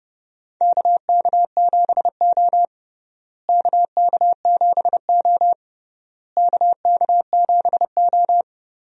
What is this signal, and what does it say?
Morse code, heard as a keyed tone, about 32 wpm, 700 Hz. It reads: KK7O KK7O KK7O
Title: USA Call Signs - 4 Characters - Repeated 3x - 30wpm